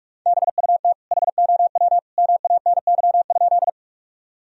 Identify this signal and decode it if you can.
Morse code, heard as a keyed tone, about 46 wpm, 700 Hz. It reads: BUT HOW MANY?
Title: Sentences from Top 1000 Words - Review Part 1 - 45wpm